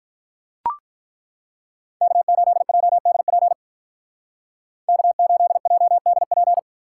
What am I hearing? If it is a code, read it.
Morse code, heard as a keyed tone, about 43 wpm, 700 Hz. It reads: K8JDP K8JDP
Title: Speed Racing - USA Call Signs 35wpm